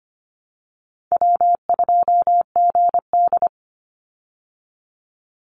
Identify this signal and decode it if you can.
Morse code, heard as a keyed tone, about 25 wpm, 700 Hz. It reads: W2GD
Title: Easy Call Signs Repeated 2x - 25wpm